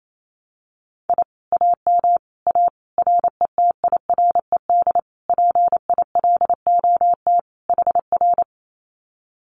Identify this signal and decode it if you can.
Morse code, heard as a keyed tone, about 28 wpm, 700 Hz. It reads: I AM A RETIRED PILOT HR
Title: Common QSO Phrases Encore 28wpm